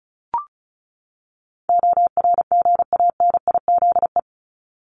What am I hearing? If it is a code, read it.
Morse code, heard as a keyed tone, about 34 wpm, 700 Hz. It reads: ORGANIZE
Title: Top 1751-2000 Words 35wpm